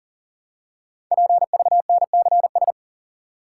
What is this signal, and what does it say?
Morse code, heard as a keyed tone, about 39 wpm, 700 Hz. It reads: PVNCS
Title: Code Groups - 5 Letter:Numbers - 40wpm